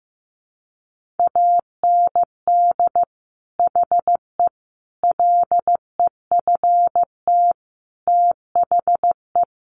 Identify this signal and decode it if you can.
Morse code, heard as a keyed tone, about 15 wpm, 700 Hz. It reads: AND HE LEFT THE
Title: Sentences from Top 300 Words 15wpm